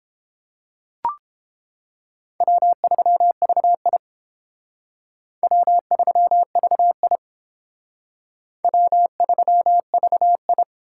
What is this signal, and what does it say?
Morse code, heard as a keyed tone, about 31 wpm, 700 Hz. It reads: W3VI W3VI W3VI